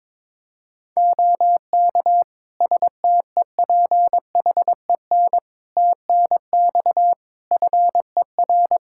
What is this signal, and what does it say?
Morse code, heard as a keyed tone, about 22 wpm, 700 Hz. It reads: OK STEPHEN TNX FER
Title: K1USN SST QSO Practice 22wpm